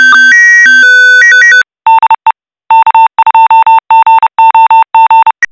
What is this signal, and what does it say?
Morse code, heard as a keyed tone, about 30 wpm, 900 Hz. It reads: DE K2GOG
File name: e9e61cbe-6188-4f30-833c / SSTV OVER FM IMAGE ONE UNDR NET